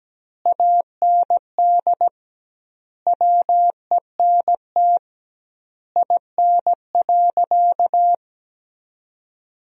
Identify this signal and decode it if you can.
Morse code, heard as a keyed tone, about 17 wpm, 700 Hz. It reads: AND WENT IN.
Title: 2x Word-Spacing - Sentences from Top 300 Words - 17wpm